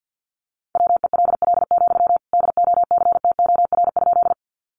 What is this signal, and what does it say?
Morse code, heard as a keyed tone, about 50 wpm, 700 Hz. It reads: WELL, DOCTOR?
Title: Sentences from Top 1500 Words 50wpm